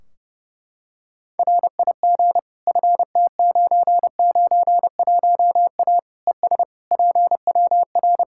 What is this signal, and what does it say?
Morse code, heard as a keyed tone, about 32 wpm, 700 Hz. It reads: RIG FT991A ES PWR
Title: K1USN SST QSO Practice 30wpm